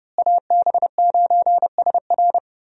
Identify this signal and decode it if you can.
Morse code, heard as a keyed tone, about 30 wpm, 700 Hz. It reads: AB9SR
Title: USA Call Signs - 5 Characters - Repeated 2x - 30wpm